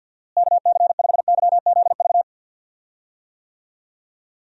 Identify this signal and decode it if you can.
Morse code, heard as a keyed tone, about 51 wpm, 700 Hz. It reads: KC5YZV
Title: USA Call Signs - 6 Characters - Repeated 3x - 50wpm